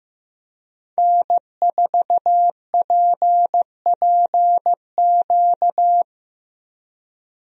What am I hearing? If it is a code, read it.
Morse code, heard as a keyed tone, about 16 wpm, 700 Hz. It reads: N4PPQ
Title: USA Call Signs - 5 Characters - Repeated 3x - 15wpm